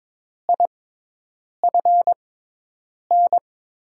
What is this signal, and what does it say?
Morse code, heard as a keyed tone, about 22 wpm, spaced as for 8 wpm, 700 Hz. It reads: IFN
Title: USA Call Signs - 6 Characters - 6x Character Spacing - 22wpm